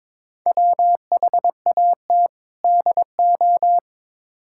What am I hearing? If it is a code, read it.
Morse code, heard as a keyed tone, about 22 wpm, 700 Hz. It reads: WHAT DO
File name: practice / 22wpm / Build A Phrase - 5-Word Phrases - 22wpm